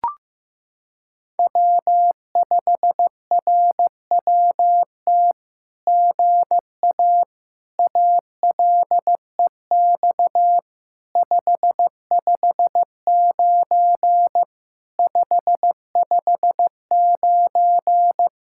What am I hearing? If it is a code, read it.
Morse code, heard as a keyed tone, about 15 wpm, 700 Hz. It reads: W5RWT GA ALEX 559 559
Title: SOTA QSO Practice 15wpm